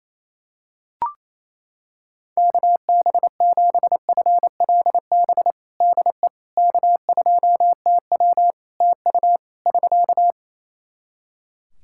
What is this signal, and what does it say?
Morse code, heard as a keyed tone, about 27 wpm, 700 Hz. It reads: KB7FLB DE K2TW TU <SK>
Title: Practice QSO - Exchange 4 - Ending - 28wpm